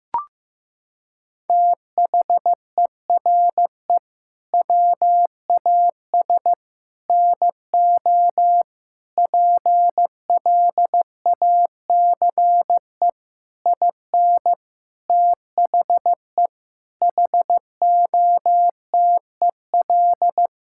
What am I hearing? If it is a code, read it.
Morse code, heard as a keyed tone, about 16 wpm, 700 Hz. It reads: THERE WAS NO PLACE IN THE HOTEL
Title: Sentences from Top 1600 Words 15wpm